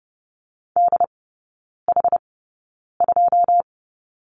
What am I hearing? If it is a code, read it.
Morse code, heard as a keyed tone, about 30 wpm, spaced as for 9 wpm, 700 Hz. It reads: DH2
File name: Lesson 18e - Letter m - Three Characters 30wpm